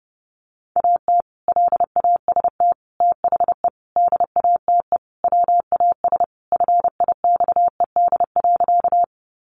Morse code, heard as a keyed tone, about 30 wpm, 700 Hz. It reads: AT LAST THE DATE WAS FIXED.
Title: Sentences from Top 1100 Words 30wpm